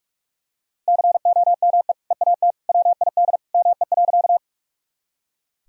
Morse code, heard as a keyed tone, about 43 wpm, 700 Hz. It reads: COME EAT WID ME.
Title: Sentences from Top 300 Words and Abbreviations 45wpm